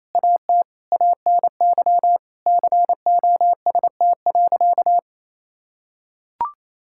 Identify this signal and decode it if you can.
Morse code, heard as a keyed tone, about 27 wpm, 700 Hz. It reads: AT ANY COST.
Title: Sentences from Top 1000 Words - Review Part 5 - 28wpm